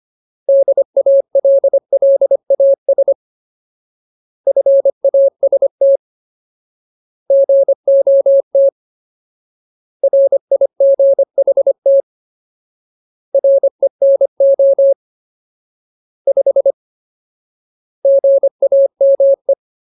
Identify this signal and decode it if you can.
Morse code, heard as a keyed tone, about 25 wpm, 550 Hz. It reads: DALLAS FAST GOT RIGHT RENO 5 GAME
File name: Ultimate Word List / 25WPM Ultimate